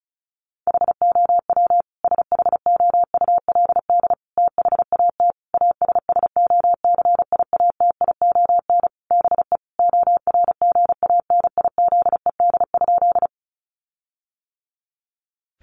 Morse code, heard as a keyed tone, about 34 wpm, 700 Hz. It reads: HOW SHOULD THAT ASSOCIATION BE ORGANIZED?
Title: Sentences from Top 1300 Words 35wpm